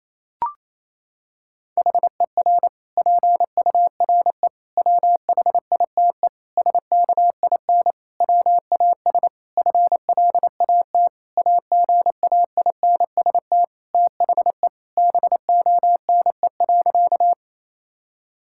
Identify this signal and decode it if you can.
Morse code, heard as a keyed tone, about 28 wpm, 700 Hz. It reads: HER PURE WHITE SKIN WAS FLAT AGAINST THE BONE.
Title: Sentences from Top 1600 Words 28wpm